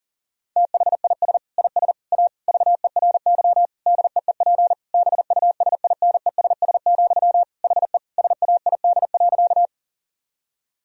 Morse code, heard as a keyed tone, about 37 wpm, 700 Hz. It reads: THIS IS A VERY DEEP BUSINESS, HE SAID.
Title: Sentences from Top 1000 Words - Review Part 3 - 40wpm